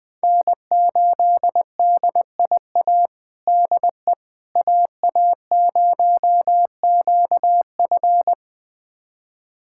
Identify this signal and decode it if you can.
Morse code, heard as a keyed tone, about 20 wpm, 700 Hz. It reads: N8DIA DE AA0QF
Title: USA Callsign DE Callsign - 5 Characters - 20wpm